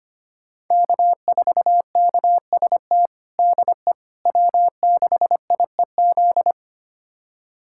Morse code, heard as a keyed tone, about 25 wpm, 700 Hz. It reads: K4KST DE W6IEZ
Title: USA Callsign DE Callsign - 5 Characters - 25wpm